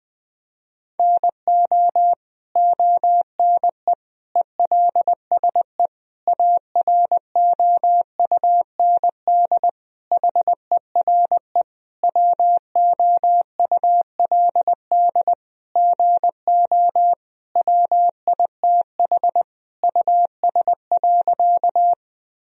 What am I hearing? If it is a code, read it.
Morse code, heard as a keyed tone, about 20 wpm, 700 Hz. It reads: NO ONE ELSE AROUND HERE WOULD GO WITH US.
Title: Sentences from Top 1000 Words - Review Part 3 - 20wpm